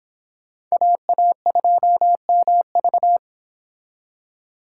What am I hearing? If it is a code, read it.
Morse code, heard as a keyed tone, about 30 wpm, 700 Hz. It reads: AA2MV